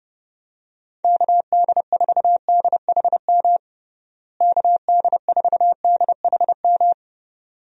Morse code, heard as a keyed tone, about 30 wpm, 700 Hz. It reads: KD4DHM KD4DHM
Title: USA Call Signs - 6 Characters - Repeated 2x - 30wpm